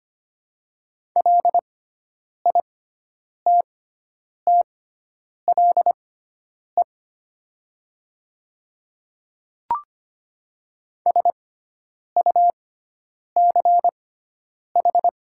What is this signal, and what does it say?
Morse code, heard as a keyed tone, about 25 wpm, spaced as for 9 wpm, 700 Hz. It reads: LITTLE SUCH
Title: 6x Character Spacing - Top 300 Words - 25wpm